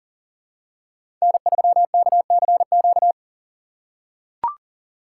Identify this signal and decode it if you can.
Morse code, heard as a keyed tone, about 40 wpm, 700 Hz. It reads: N3KCQ
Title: USA Call Signs - 5 Characters - 2x Character Spacing - 40wpm